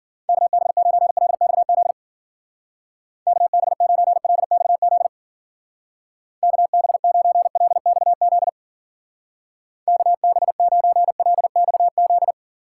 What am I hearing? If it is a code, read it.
Morse code, heard as a keyed tone, about 50 wpm, 700 Hz. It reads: KB9LXZ KB9LXZ KB9LXZ KB9LXZ